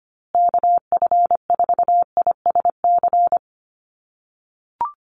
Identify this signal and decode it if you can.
Morse code, heard as a keyed tone, about 25 wpm, 700 Hz. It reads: KF4ISC
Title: USA Call Signs Repeated 3x 25wpm